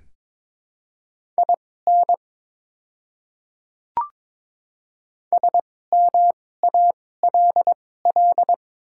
Morse code, heard as a keyed tone, about 22 wpm, spaced as for 16 wpm, 700 Hz. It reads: IN SMALL
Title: 2x Character Spacing - Top 200 Words - 22wpm